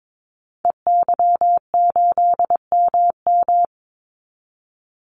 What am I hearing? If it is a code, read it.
Morse code, heard as a keyed tone, about 22 wpm, 700 Hz. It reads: EY8MM
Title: Call Signs Repeated 2x - 22wpm